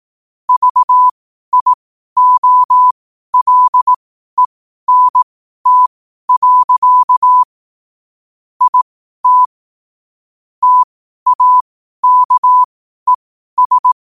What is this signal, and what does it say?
Morse code, heard as a keyed tone, about 18 wpm, spaced as for 12 wpm, 1000 Hz. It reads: VIOLENT. IT TAKES
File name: qotdfiles / 12wpm